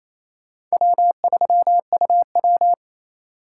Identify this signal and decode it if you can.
Morse code, heard as a keyed tone, about 35 wpm, 700 Hz. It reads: W3UW